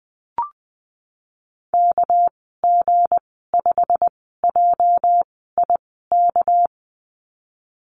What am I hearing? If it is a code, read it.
Morse code, heard as a keyed tone, about 20 wpm, spaced as for 14 wpm, 700 Hz. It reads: KG5JIK
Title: 245 Lesson 39g - Letter z - Call Signs 20wpm